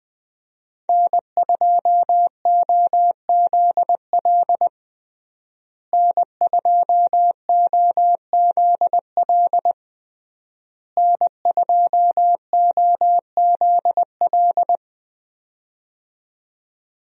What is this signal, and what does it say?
Morse code, heard as a keyed tone, about 21 wpm, 700 Hz. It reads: N2OZL N2OZL N2OZL
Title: USA Call Signs - 5 Characters - Repeated 3x - 20wpm